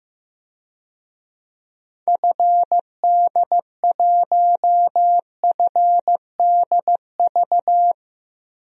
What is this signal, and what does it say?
Morse code, heard as a keyed tone, about 16 wpm, 700 Hz. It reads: FD1FDV
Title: DXCC-100 Call Signs - 6 Characters - Repeated 3x - 15wpm